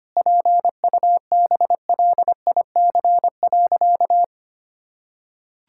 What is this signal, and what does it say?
Morse code, heard as a keyed tone, about 25 wpm, 700 Hz. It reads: PUBLIC.
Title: Sentences from Top 1000 Words - Review Part 6 - 25wpm